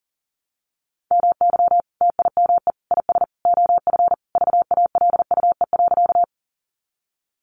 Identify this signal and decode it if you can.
Morse code, heard as a keyed tone, about 40 wpm, 700 Hz. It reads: MY TIME IS OF VALUE.
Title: Sentences from Top 600 Words 40wpm